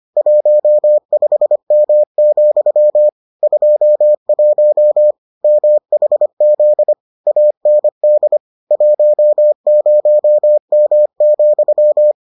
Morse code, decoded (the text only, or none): 15M, 21 MHZ AND 10M,